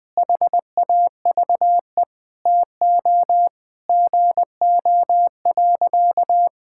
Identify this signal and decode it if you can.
Morse code, heard as a keyed tone, about 20 wpm, 700 Hz. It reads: HAVE TO GO.
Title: Sentences with 4-Word Phrases - 20wpm